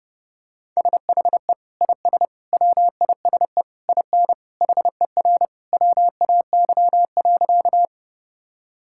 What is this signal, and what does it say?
Morse code, heard as a keyed tone, about 30 wpm, 700 Hz. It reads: SHE IS WISE IN HER WAY.
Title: Sentences from Top 1600 Words 30wpm